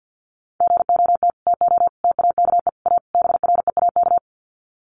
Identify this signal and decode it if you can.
Morse code, heard as a keyed tone, about 50 wpm, 700 Hz. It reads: GOT TO TAKE A BREAK
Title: Sets of 5 Words Encore 50wpm